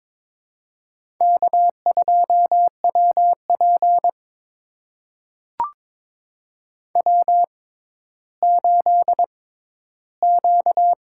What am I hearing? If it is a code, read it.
Morse code, heard as a keyed tone, about 22 wpm, spaced as for 13 wpm, 700 Hz. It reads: K2WP W8Q
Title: USA Call Signs - 4 Characters - 6x Character Spacing - 22wpm